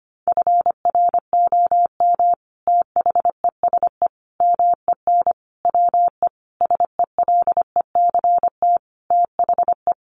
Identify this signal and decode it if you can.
Morse code, heard as a keyed tone, about 25 wpm, 700 Hz. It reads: FROM THESE MEN WE SELECT THE
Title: Sentences from Top 1300 Words 25wpm